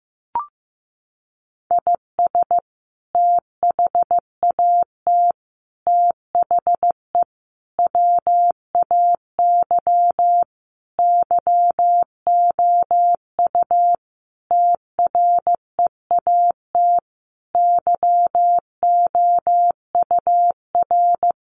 Morse code, heard as a keyed tone, about 15 wpm, 700 Hz. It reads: IS THAT THE WAY YOU TREAT YOUR
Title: Sentences from Top 1300 Words 15wpm